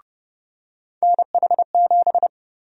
Morse code, heard as a keyed tone, about 30 wpm, 700 Hz. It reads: NH7
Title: USA Call Sign Prefixes - 30wpm